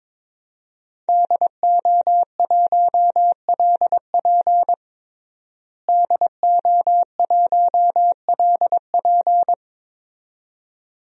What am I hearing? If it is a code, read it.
Morse code, heard as a keyed tone, about 22 wpm, 700 Hz. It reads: DO1LP DO1LP
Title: DXCC-100 Call Signs - 5 Characters - Repeated 2x - 22wpm